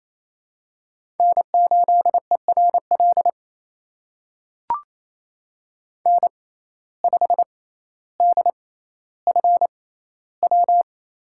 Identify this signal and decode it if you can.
Morse code, heard as a keyed tone, about 27 wpm, spaced as for 14 wpm, 700 Hz. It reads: N8ERL N5DFW
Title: USA Call Signs - 5 Characters - 6x Character Spacing - 28wpm